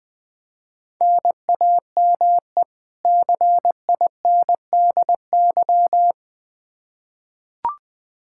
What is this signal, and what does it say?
Morse code, heard as a keyed tone, about 20 wpm, 700 Hz. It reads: NAME CINDY
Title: K1USN SST QSO Practice 20wpm